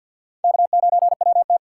Morse code, heard as a keyed tone, about 48 wpm, 700 Hz. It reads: K9WT
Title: USA Call Signs - 4 Characters - 6x Character Spacing - 50wpm